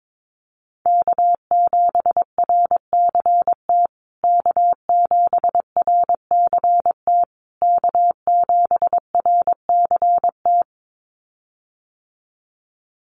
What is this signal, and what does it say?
Morse code, heard as a keyed tone, about 22 wpm, 700 Hz. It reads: K7RCT K7RCT K7RCT
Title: USA Call Signs Repeated 3x 22wpm